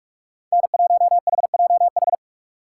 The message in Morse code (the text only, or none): N1HJH